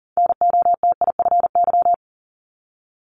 Morse code, heard as a keyed tone, about 40 wpm, 700 Hz. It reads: NOTIFY
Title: Multi-letter Phonemes Repeated 3x 40wpm